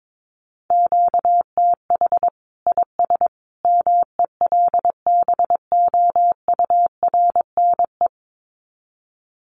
Morse code, heard as a keyed tone, about 22 wpm, 700 Hz. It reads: QTH IS MELBOURNE
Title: Common QSO Phrases 22wpm